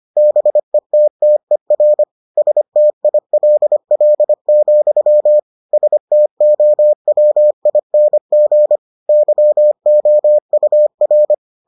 Morse code, heard as a keyed tone, about 25 wpm, 600 Hz. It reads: BETTER STILL, STOWING YOUR